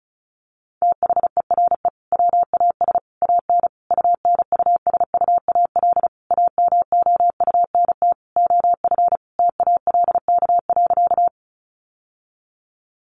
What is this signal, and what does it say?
Morse code, heard as a keyed tone, about 34 wpm, 700 Hz. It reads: THERE WAS AN UNUSUAL AMOUNT OF TALK.
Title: Sentences from Top 1400 Words 35wpm